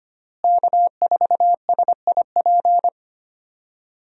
Morse code, heard as a keyed tone, about 25 wpm, 700 Hz. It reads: K4SIP
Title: USA Call Signs - 5 Characters - Repeated 2x - 25wpm